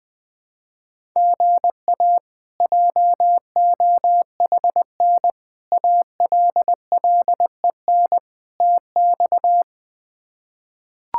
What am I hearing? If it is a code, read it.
Morse code, heard as a keyed tone, about 20 wpm, 700 Hz. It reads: GA JOHN ALLEN TX
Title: K1USN SST QSO Practice - 20wpm